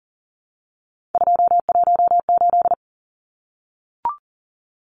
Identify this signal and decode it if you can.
Morse code, heard as a keyed tone, about 40 wpm, 700 Hz. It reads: 218
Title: Numbers - 3-Digit 40wpm